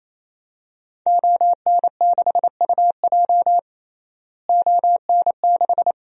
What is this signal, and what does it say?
Morse code, heard as a keyed tone, about 28 wpm, 700 Hz. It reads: ON6UJ ON6
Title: DXCC-100 Call Signs - 5 Characters - Repeated 3x - 28wpm